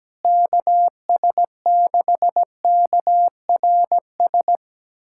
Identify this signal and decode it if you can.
Morse code, heard as a keyed tone, about 17 wpm, 700 Hz. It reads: KS6KRS
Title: USA Callsign DE Callsign 17wpm